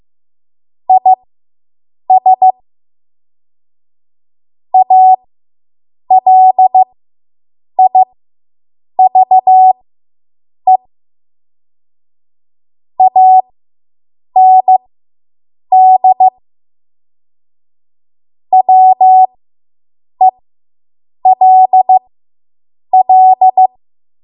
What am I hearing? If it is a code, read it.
Morse code, heard as a keyed tone, about 15 wpm, spaced as for 7 wpm, 750 Hz. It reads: IS ALIVE AND WELL